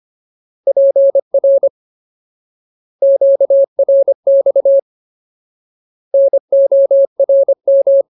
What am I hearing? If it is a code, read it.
Morse code, heard as a keyed tone, about 25 wpm, 550 Hz. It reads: PR QRX NORM
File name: Ultimate Word List / 25WPM Ultimate